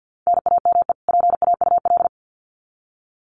Morse code, heard as a keyed tone, about 50 wpm, 700 Hz. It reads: NAME PAUL
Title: Common QSO Phrases 50wpm